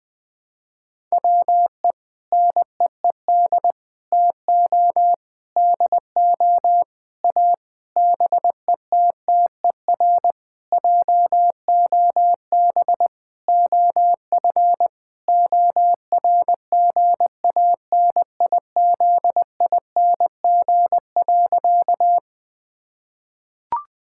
Morse code, heard as a keyed tone, about 19 wpm, 700 Hz. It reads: WE NEED TO DO A BETTER JOB OF ORGANIZING.
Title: Sentences with 7-Word Phrases - 20wpm